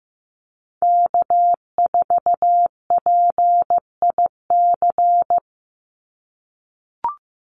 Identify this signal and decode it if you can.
Morse code, heard as a keyed tone, about 15 wpm, 700 Hz. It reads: K4PIC
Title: NA SOTA Chaser Callsigns 15wpm